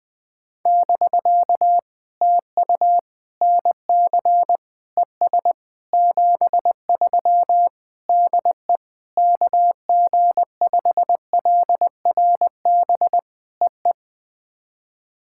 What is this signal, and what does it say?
Morse code, heard as a keyed tone, about 20 wpm, 700 Hz. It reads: <BK> TU NC ES 73 DE KG5LRB EE
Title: POTA QSO Practice - 20wpm